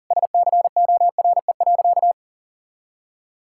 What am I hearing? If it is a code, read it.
Morse code, heard as a keyed tone, about 40 wpm, 700 Hz. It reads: SCORE.
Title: Sentences from Top 1600 Words 40wpm